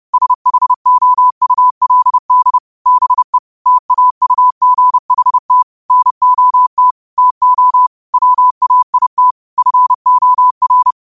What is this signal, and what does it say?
Morse code, heard as a keyed tone, about 30 wpm, 1000 Hz. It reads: SHOULD BE TAUGHT NOT TO WAIT FOR
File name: qotdfiles / 30wpm